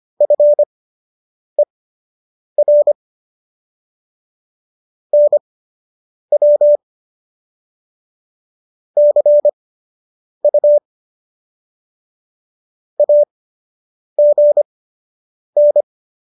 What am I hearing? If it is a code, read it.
Morse code, heard as a keyed tone, about 25 wpm, spaced as for 8 wpm, 590 Hz. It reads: FER NW CU AGN